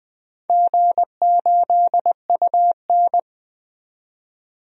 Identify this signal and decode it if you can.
Morse code, heard as a keyed tone, about 21 wpm, 700 Hz. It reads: G8UN
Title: DXCC-100 Call Signs - 4 Characters - 20wpm